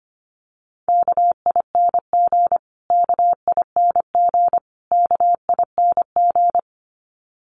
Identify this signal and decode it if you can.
Morse code, heard as a keyed tone, about 25 wpm, 700 Hz. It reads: KING KING KING
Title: Top 500 Words Repeated 25wpm